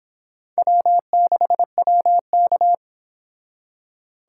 Morse code, heard as a keyed tone, about 30 wpm, 700 Hz. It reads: W6WK